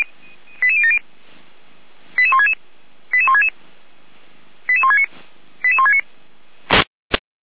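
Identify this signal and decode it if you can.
Five-tone selective call (ZVEI), heard as quick runs of five tones, tones of about 70 ms. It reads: 88070 88166 88166 88169 88169
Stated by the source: ZVEI five-tone selective call 88124 received by SDR, from Feuerwehr Orlamünde.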